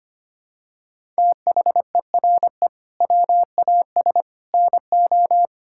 Morse code, heard as a keyed tone, about 25 wpm, 700 Hz. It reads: THERE WAS NO
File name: practice / 25wpm / Sentences with 3-Word Phrases - 25wpm